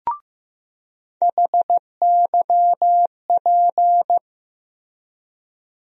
Morse code, heard as a keyed tone, about 15 wpm, 700 Hz. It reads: HYP
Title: USA Call Sign Postfixes - 3 Letters 15wpm